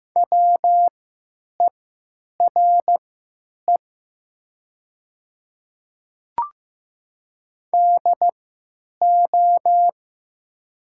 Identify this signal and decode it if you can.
Morse code, heard as a keyed tone, about 15 wpm, spaced as for 9 wpm, 700 Hz. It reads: WERE DO
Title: 3x Character Spacing - Top 300 Words - 15wpm